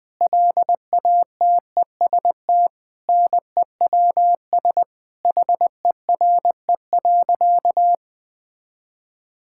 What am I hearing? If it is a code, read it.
Morse code, heard as a keyed tone, about 20 wpm, 700 Hz. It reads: LATEST NEWS HERE.
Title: Sentences with 9-Word Phrases - 20wpm